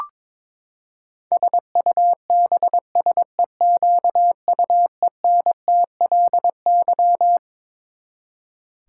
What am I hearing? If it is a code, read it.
Morse code, heard as a keyed tone, about 22 wpm, 700 Hz. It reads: SUBSEQUENTLY
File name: Twelve Letter Words - 22wpm